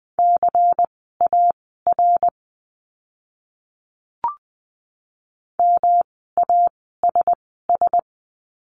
Morse code, heard as a keyed tone, about 20 wpm, spaced as for 14 wpm, 700 Hz. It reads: CAR MASS
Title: 108 Lesson 18f - Letter m - Words 20wpm